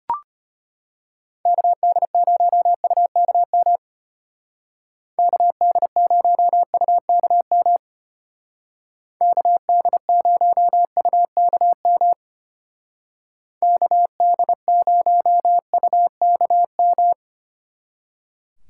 Morse code, heard as a keyed tone, about 32 wpm, 700 Hz. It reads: KD0UKM KD0UKM KD0UKM KD0UKM